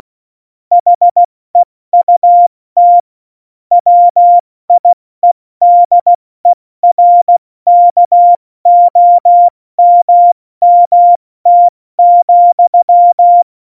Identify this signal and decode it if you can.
Morse code, heard as a keyed tone, about 16 wpm, spaced as for 14 wpm, 700 Hz. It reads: HEUT WIEDERKOMMT,
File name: Morsekurs / Witze / Witz 16. April 2019, 70 F80